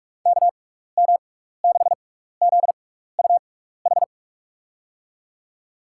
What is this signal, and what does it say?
Morse code, heard as a keyed tone, about 46 wpm, spaced as for 16 wpm, 700 Hz. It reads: KM6ZUH
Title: USA Call Signs - 6 Characters - 6x Character Spacing - 45wpm